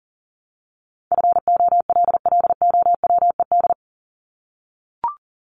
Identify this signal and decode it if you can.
Morse code, heard as a keyed tone, about 40 wpm, 700 Hz. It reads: FOLLOWED
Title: ED Suffix - Top 100 Words with Base Words 40wpm